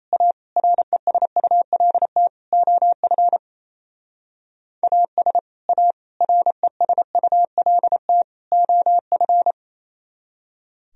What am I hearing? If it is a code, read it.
Morse code, heard as a keyed tone, about 34 wpm, 700 Hz. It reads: A RESULT OF AS A RESULT OF
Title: Speed-Racing - Sets of 4 Words 28wpm